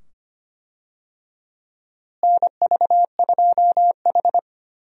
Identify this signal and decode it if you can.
Morse code, heard as a keyed tone, about 25 wpm, 700 Hz. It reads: NV2H
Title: USA Call Signs - 4 Characters - Repeated 2x - 25wpm